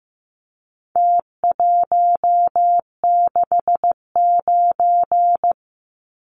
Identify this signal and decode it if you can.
Morse code, heard as a keyed tone, about 15 wpm, 700 Hz. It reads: T169
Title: Numbers - Contest Style - 4-Digit 15wpm